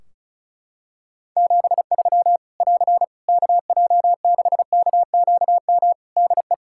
Morse code, heard as a keyed tone, about 35 wpm, 700 Hz. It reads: 73 <AR> KJ6KQM DE
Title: Practice QSO - Exchange 4 - Ending - 35wpm